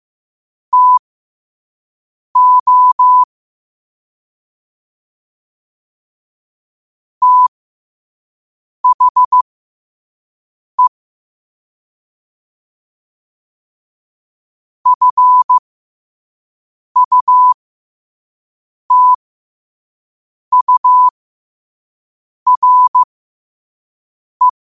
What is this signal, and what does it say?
Morse code, heard as a keyed tone, about 15 wpm, spaced as for 5 wpm, 1000 Hz. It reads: TO THE FUTURE
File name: qotdfiles / 5wpm